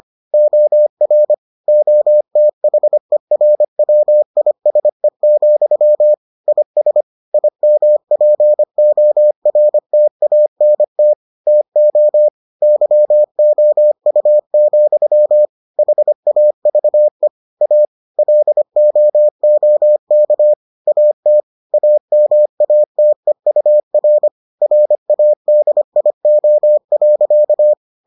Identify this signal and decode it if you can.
Morse code, heard as a keyed tone, about 25 wpm, 600 Hz. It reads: OR OTHERWISE, IS IMPORTANT TO YOU, HAVE A LOOK AT AMATEUR RADIO.